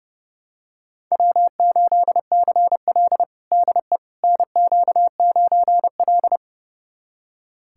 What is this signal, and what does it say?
Morse code, heard as a keyed tone, about 30 wpm, 700 Hz. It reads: W8CL DE NQ9L
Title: USA Callsign DE Callsign - 4 Characters - 30wpm